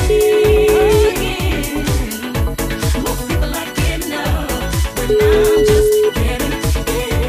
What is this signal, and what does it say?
Ringback tone (Europe, 425 Hz), with noise.